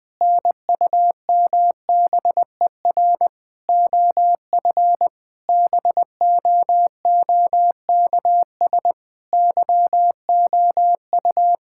Morse code, heard as a keyed tone, about 21 wpm, 700 Hz. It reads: NUMBER OF BOOKS YOU
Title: Sentences with 8-Word Phrases - 20wpm